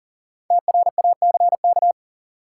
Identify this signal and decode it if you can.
Morse code, heard as a keyed tone, about 40 wpm, 700 Hz. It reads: TRACK
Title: Sentences from Top 1100 Words - New Word Drill Alt - 40wpm